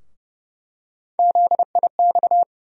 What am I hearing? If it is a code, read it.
Morse code, heard as a keyed tone, about 30 wpm, 700 Hz. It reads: ZIX
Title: USA Call Sign Postfixes - 3 Letters 30wpm